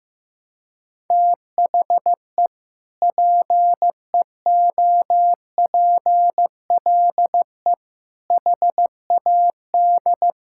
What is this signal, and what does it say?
Morse code, heard as a keyed tone, about 16 wpm, 700 Hz. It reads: THE PEOPLE HAD
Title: Sentences from Top 1000 Words - Review Part 6 - 15wpm